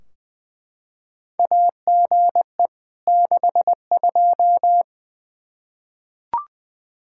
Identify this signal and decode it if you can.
Morse code, heard as a keyed tone, about 20 wpm, 700 Hz. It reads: AGE 62
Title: K1USN SST QSO Practice 20wpm